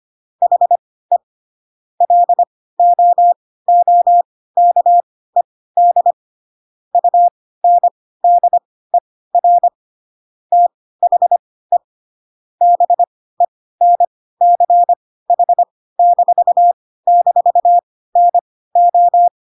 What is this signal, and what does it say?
Morse code, heard as a keyed tone, about 25 wpm, spaced as for 16 wpm, 700 Hz. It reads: HE LOOKED UNDER THE BENCH--NO
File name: Pinocchio / Pinocchio ch1 10 25 ik1hge